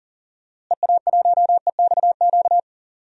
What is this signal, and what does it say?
Morse code, heard as a keyed tone, about 40 wpm, 700 Hz. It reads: EA1EXQ
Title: DXCC-100 Call Signs - 6 Characters - Repeated 2x - 40wpm